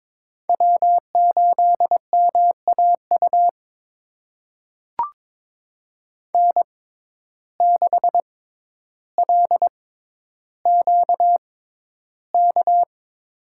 Morse code, heard as a keyed tone, about 23 wpm, spaced as for 11 wpm, 700 Hz. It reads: W8MAU N6LQK
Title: USA Call Signs - 5 Characters - 6x Character Spacing - 22wpm